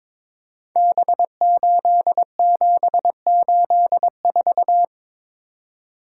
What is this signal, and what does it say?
Morse code, heard as a keyed tone, about 22 wpm, 700 Hz. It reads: B8784
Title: Code Groups - 5 Letter:Numbers - 22wpm